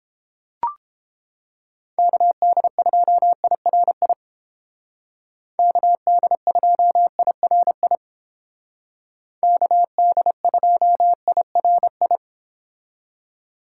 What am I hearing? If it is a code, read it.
Morse code, heard as a keyed tone, about 29 wpm, 700 Hz. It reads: KD2IRI KD2IRI KD2IRI